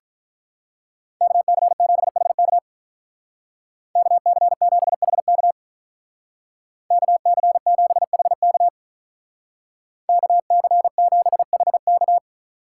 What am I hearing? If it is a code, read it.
Morse code, heard as a keyed tone, about 44 wpm, 700 Hz. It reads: KC7HK KC7HK KC7HK KC7HK